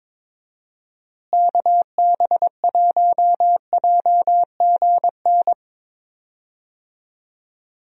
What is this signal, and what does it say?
Morse code, heard as a keyed tone, about 22 wpm, 700 Hz. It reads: KB1JGN
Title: USA Call Signs - 6 Characters - Repeated 3x - 22wpm